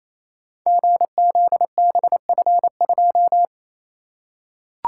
Morse code, heard as a keyed tone, about 29 wpm, 700 Hz. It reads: GZBF2
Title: Code Groups - 5 Letter:Numbers - 28wpm